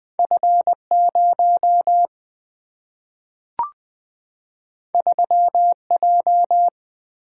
Morse code, heard as a keyed tone, about 20 wpm, 700 Hz. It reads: F0 3J
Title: Two Letter-Number Combinations - 20wpm